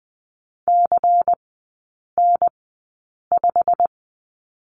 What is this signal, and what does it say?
Morse code, heard as a keyed tone, about 20 wpm, spaced as for 8 wpm, 690 Hz. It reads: CN5
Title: 145 Lesson 24e - Letter y - Three Characters 20wpm